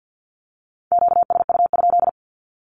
Morse code, heard as a keyed tone, about 50 wpm, 700 Hz. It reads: QSU?
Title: Q-Codes 50wpm